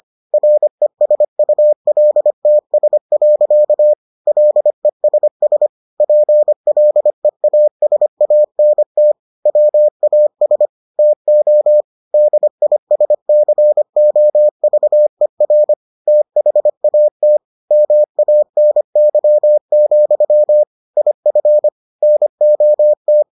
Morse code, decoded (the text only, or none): RESULTS. LESS PLEASANT WAS TO DISCOVER THAT MANY, IF NOT